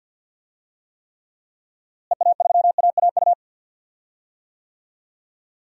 Morse code, heard as a keyed tone, about 51 wpm, 700 Hz. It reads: EA3AAU
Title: DXCC-100 Call Signs - 6 Characters - Repeated 3x - 50wpm